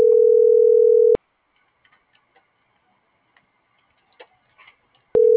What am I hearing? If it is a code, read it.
Ringback tone (North America), 440 + 480 Hz.